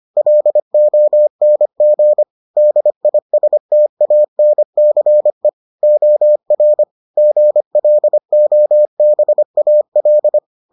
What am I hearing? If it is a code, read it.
Morse code, heard as a keyed tone, about 25 wpm, 600 Hz. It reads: LONG DISTANCE OR GLOBAL